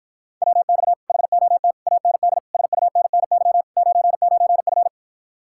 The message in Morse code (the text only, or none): WX HOT AND SUNNY 99F